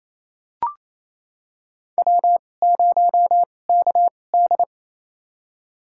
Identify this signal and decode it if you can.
Morse code, heard as a keyed tone, about 28 wpm, spaced as for 20 wpm, 700 Hz. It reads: W0KD
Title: USA Call Signs - 4 Characters - 2x Character Spacing - 28wpm